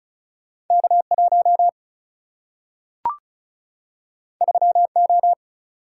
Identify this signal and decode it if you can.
Morse code, heard as a keyed tone, about 36 wpm, 700 Hz. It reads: K 1 3 O
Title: Single-Letter Single-Number 35wpm